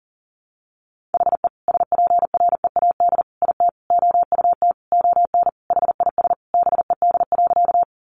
Morse code, heard as a keyed tone, about 37 wpm, 700 Hz. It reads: HE SPREAD IT OUT ON HIS BED.
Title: Sentences from Top 1200 Words 40wpm